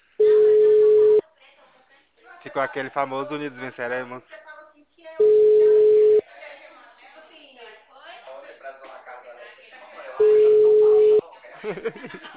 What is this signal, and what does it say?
Ringback tone (Europe, 425 Hz), with speech.
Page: Goiania - GO